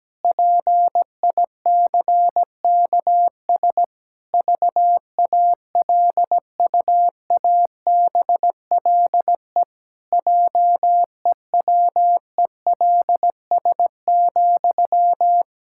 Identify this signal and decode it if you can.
Morse code, heard as a keyed tone, about 17 wpm, 700 Hz. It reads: PICKS VALUABLE JEWELS,
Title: Pangram Challenge v2 - 17wpm